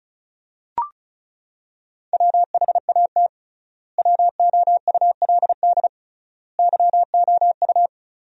Morse code, heard as a keyed tone, about 36 wpm, 700 Hz. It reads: WHAT WOULD YOU
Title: 3x Word-Spacing - Sentences from Top 300 Words - 35wpm